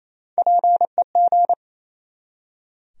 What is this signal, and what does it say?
Morse code, heard as a keyed tone, about 28 wpm, 700 Hz. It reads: PEG
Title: USA Common Names - Max 4 Letters - 28wpm